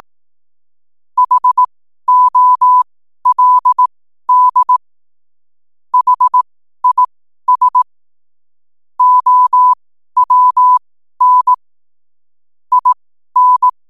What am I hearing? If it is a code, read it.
Morse code, heard as a keyed tone, about 18 wpm, spaced as for 12 wpm, 1000 Hz. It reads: HOLD HIS OWN IN
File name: n7net / 12wpm